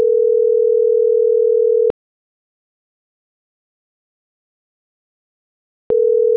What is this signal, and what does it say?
Ringback tone (North America), 440 + 480 Hz.